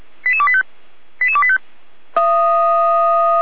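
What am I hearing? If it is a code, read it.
Five-tone selective call (ZVEI), heard as quick runs of five tones, tones of about 70 ms. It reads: 88285 88285
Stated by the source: ZVEI five-tone call received on SDR.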